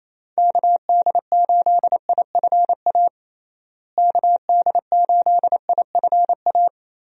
Morse code, heard as a keyed tone, about 28 wpm, 700 Hz. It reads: KD8IFA KD8IFA
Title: USA Call Signs - 6 Characters - Repeated 3x - 28wpm